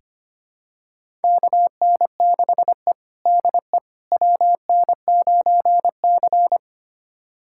Morse code, heard as a keyed tone, about 25 wpm, 700 Hz. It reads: KN6E DE WN9C
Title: USA Callsign DE Callsign - 4 Characters - 25wpm